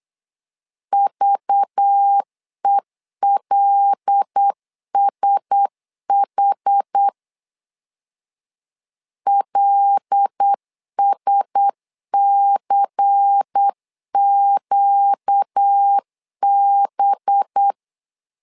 Morse code, heard as a keyed tone, about 8 wpm, 800 Hz. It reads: VELSH LSCQB
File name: LEZIONI / 5 LBVC / 5 LBVC 8wpm 2935